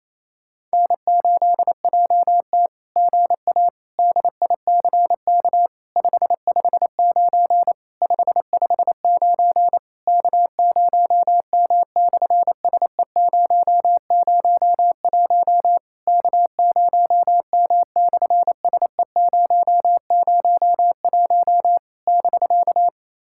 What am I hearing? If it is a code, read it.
Morse code, heard as a keyed tone, about 27 wpm, 700 Hz. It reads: N8JT GA DICK 559 559 K0M/SE001 K0M/SE001 <BK>
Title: SOTA QSO Practice 28wpm